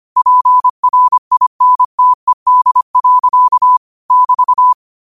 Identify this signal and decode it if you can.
Morse code, heard as a keyed tone, about 25 wpm, 1000 Hz. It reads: PRINTED. <BT>
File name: qotdfiles / 25wpm